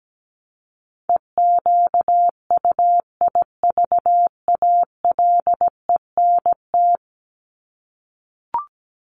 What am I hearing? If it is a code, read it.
Morse code, heard as a keyed tone, about 17 wpm, 700 Hz. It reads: EQUIVALENT
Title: Top 1501-1750 Words 17wpm